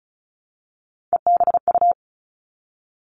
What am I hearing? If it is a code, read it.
Morse code, heard as a keyed tone, about 34 wpm, 700 Hz. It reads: EBU
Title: All Cut Numbers - 3-Digit 35wpm